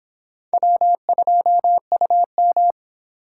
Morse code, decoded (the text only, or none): W2UM